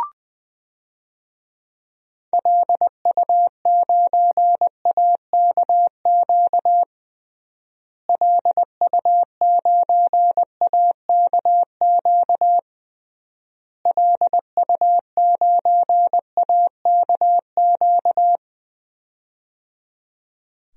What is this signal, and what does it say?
Morse code, heard as a keyed tone, about 20 wpm, 700 Hz. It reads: LU9AKQ LU9AKQ LU9AKQ
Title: DXCC-100 Call Signs - 6 Characters - Repeated 3x - 20wpm